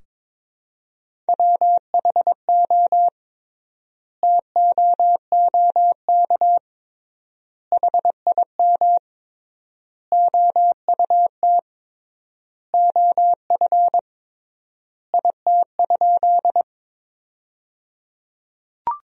Morse code, heard as a keyed tone, about 22 wpm, 700 Hz. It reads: WHO TOOK HIM OUT OF IT?
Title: 3x Word-Spacing - Sentences from Top 300 Words - 22wpm